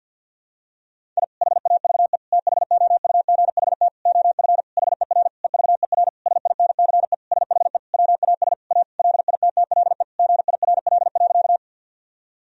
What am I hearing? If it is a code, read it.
Morse code, heard as a keyed tone, about 52 wpm, 700 Hz. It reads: I HAVE THOUGHT OF HER EVER SINCE SHE WAS A LITTLE GIRL.
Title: Sentences from Top 1000 Words - Review Part 1 - 50wpm